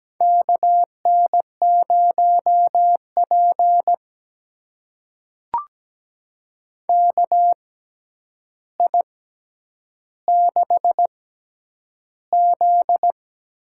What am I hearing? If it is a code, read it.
Morse code, heard as a keyed tone, about 17 wpm, spaced as for 9 wpm, 700 Hz. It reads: KN0P KI6Z
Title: USA Call Signs - 6x Character Spacing - 17wpm